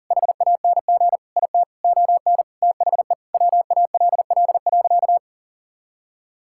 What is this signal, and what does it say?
Morse code, heard as a keyed tone, about 37 wpm, 700 Hz. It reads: HANG IT ON THE WALL.
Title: Sentences from Top 1600 Words 40wpm